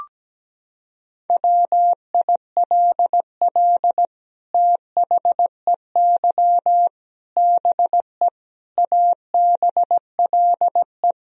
Morse code, heard as a keyed tone, about 17 wpm, 700 Hz. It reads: WILL THEY BE ABLE
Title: Sentences from Top 1000 Words - Review Part 3 - 17wpm